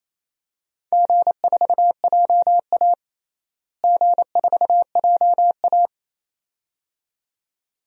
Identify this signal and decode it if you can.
Morse code, heard as a keyed tone, about 28 wpm, 700 Hz. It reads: G4JA G4JA
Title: DXCC-100 Call Signs - 4 Characters - Repeated 2x - 28wpm